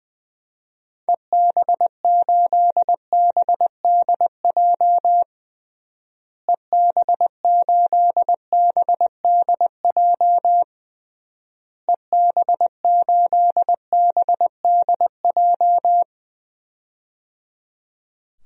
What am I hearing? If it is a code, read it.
Morse code, heard as a keyed tone, about 20 wpm, 700 Hz. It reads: EB8BDJ EB8BDJ EB8BDJ
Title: DXCC-100 Call Signs - 6 Characters - Repeated 3x - 20wpm